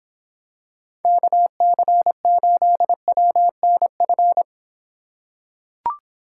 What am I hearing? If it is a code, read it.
Morse code, heard as a keyed tone, about 26 wpm, 700 Hz. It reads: KC8WNF